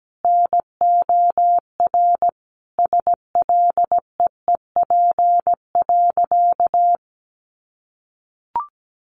Morse code, heard as a keyed tone, about 17 wpm, 700 Hz. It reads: NOR SLEEP.
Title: Sentences from Top 1500 Words 17wpm